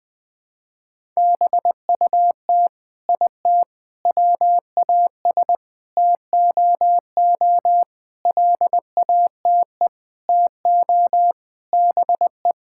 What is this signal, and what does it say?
Morse code, heard as a keyed tone, about 21 wpm, 700 Hz. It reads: BUT IT WAS TOO LATE TO BE
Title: Sentences from Top 1600 Words 20wpm